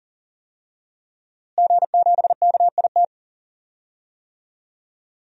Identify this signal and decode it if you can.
Morse code, heard as a keyed tone, about 40 wpm, 700 Hz. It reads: G7KIT
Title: DXCC-100 Call Signs - 5 Characters - Repeated 2x - 40wpm